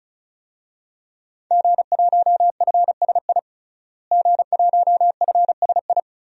Morse code, heard as a keyed tone, about 35 wpm, 700 Hz. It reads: G1FSI G1FSI
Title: DXCC-100 Call Signs - 5 Characters - Repeated 2x - 35wpm